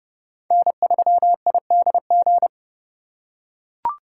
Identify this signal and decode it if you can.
Morse code, heard as a keyed tone, about 27 wpm, 700 Hz. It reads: N3IDG